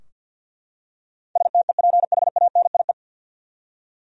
Morse code, heard as a keyed tone, about 53 wpm, 700 Hz. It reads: STEPHANIE
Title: USA Common Names - 50wpm